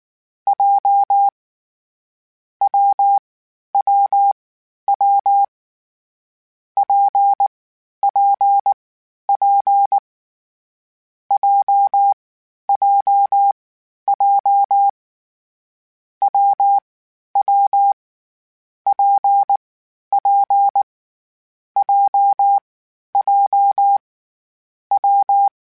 Morse code, heard as a keyed tone, about 19 wpm, 800 Hz. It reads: J W W W P P P J J J W W P P J J W
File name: Difficult Combos / Difficult Combos-mp3s / WPJ WPJ1 @18